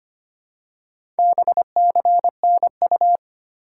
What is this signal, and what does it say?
Morse code, heard as a keyed tone, about 25 wpm, 700 Hz. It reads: BCNU
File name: Top 90 QSO Elements - 3x Character Spacing - 25wpm